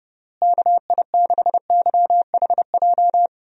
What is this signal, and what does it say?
Morse code, heard as a keyed tone, about 30 wpm, 700 Hz. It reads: KI6YHJ
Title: USA Call Signs - 6 Characters - Repeated 2x - 30wpm